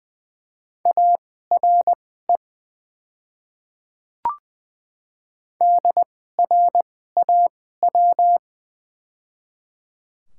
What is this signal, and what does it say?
Morse code, heard as a keyed tone, about 20 wpm, spaced as for 14 wpm, 700 Hz. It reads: ARE DRAW
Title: 2x Character Spacing - Top 300 Words - 20wpm